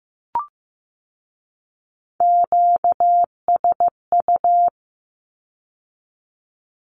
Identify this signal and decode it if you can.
Morse code, heard as a keyed tone, about 15 wpm, 700 Hz. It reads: QSU
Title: Q-Codes with Spoken Meaning - 15wpm